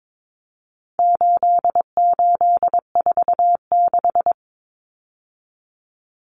Morse code, heard as a keyed tone, about 22 wpm, 700 Hz. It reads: 8846
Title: Numbers - 4-Digit 22wpm